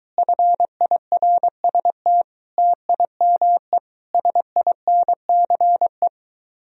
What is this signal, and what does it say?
Morse code, heard as a keyed tone, about 23 wpm, 700 Hz. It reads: FIRST TIME SINCE